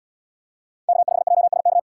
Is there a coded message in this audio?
Yes: Morse code from 0.9 to 1.8 s, heard as a keyed tone, about 73 wpm, 700 Hz. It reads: K5JSK